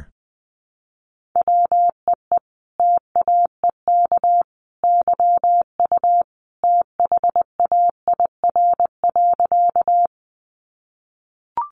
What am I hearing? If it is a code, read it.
Morse code, heard as a keyed tone, about 20 wpm, 700 Hz. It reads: WEE TAEK YU THAIR.
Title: A Sound Of Thunder 20wpm